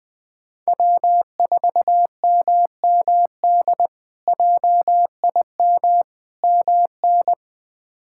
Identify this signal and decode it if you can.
Morse code, heard as a keyed tone, about 20 wpm, 700 Hz. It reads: W4MMD JIM MN
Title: K1USN SST QSO Practice - 20wpm